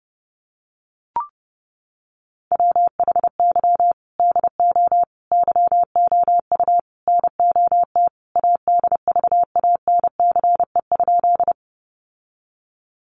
Morse code, heard as a keyed tone, about 30 wpm, 700 Hz. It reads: WHY DO YOU NOT ADVANCE?
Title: Sentences from Top 1400 Words 30wpm